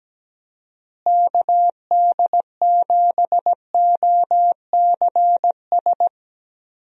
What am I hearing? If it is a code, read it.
Morse code, heard as a keyed tone, about 17 wpm, 700 Hz. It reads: KD7OCS
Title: USA Call Signs - 6 Characters - Repeated 3x - 17wpm